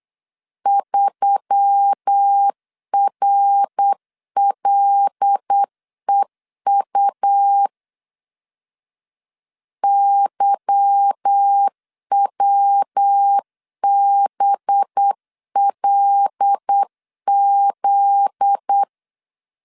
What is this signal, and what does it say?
Morse code, heard as a keyed tone, about 8 wpm, 800 Hz. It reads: 3RLEU YWBLZ
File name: LEZIONI / 8 YJ45 / 8 Y 8wpm 3245